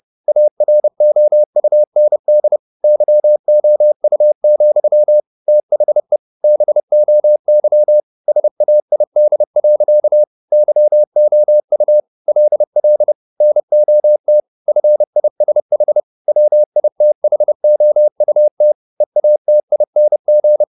Morse code, heard as a keyed tone, about 30 wpm, 600 Hz. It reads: AROUND YOU, THE BOY SAID. YOU LL NOT FISH WITHOUT EATING
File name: the old man and the sea / the old man and the sea 1 30wpm